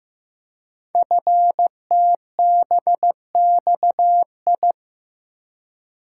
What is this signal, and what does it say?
Morse code, heard as a keyed tone, about 15 wpm, 700 Hz. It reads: FTBXI
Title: Code Groups - 5 Letters - 15wpm